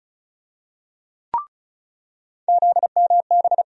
Morse code, heard as a keyed tone, about 36 wpm, 700 Hz. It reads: ZMB
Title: USA Call Sign Postfixes - 3 Letters 35wpm